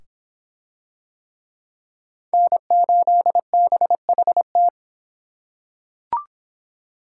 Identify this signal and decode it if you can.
Morse code, heard as a keyed tone, about 25 wpm, 700 Hz. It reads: N8BHT